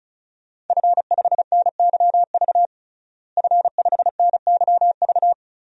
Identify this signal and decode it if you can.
Morse code, heard as a keyed tone, about 36 wpm, 700 Hz. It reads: F5NYV F5NYV
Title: DXCC-100 Call Signs - 5 Characters - Repeated 3x - 35wpm